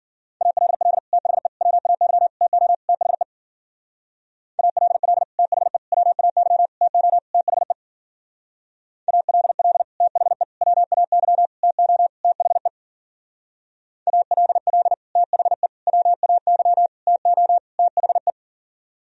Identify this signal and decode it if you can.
Morse code, heard as a keyed tone, about 52 wpm, 700 Hz. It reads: ALL THE WAY TO THE ALL THE WAY TO THE ALL THE WAY TO THE ALL THE WAY TO THE